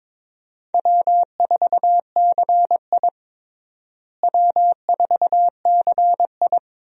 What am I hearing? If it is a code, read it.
Morse code, heard as a keyed tone, about 22 wpm, 700 Hz. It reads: W4CI W4CI
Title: USA Call Signs - 4 Characters - Repeated 2x - 22wpm